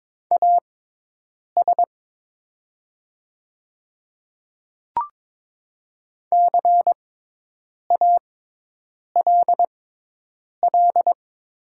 Morse code, heard as a keyed tone, about 22 wpm, spaced as for 8 wpm, 700 Hz. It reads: AS CALL
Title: 6x Character Spacing - Top 200 Words - 22wpm